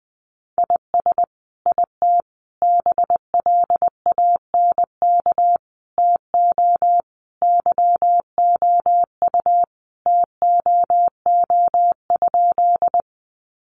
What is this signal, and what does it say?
Morse code, heard as a keyed tone, about 19 wpm, 700 Hz. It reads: IS IT BLANK TO YOU TOO?
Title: Sentences from Top 1500 Words 20wpm